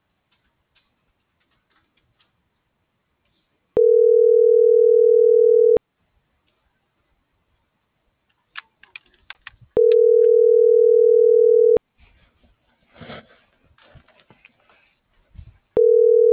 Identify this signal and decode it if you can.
Ringback tone (North America), 440 + 480 Hz.